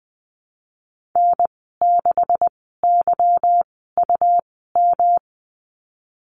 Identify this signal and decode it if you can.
Morse code, heard as a keyed tone, about 19 wpm, spaced as for 14 wpm, 700 Hz. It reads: N6YUM
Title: 154 Lesson 25g - Letter p - Call Signs 20wpm